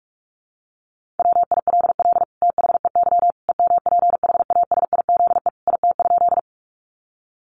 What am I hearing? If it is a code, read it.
Morse code, heard as a keyed tone, about 44 wpm, 700 Hz. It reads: WILL THEY EMPHASIZE IT?
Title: Sentences from Top 1500 Words 45wpm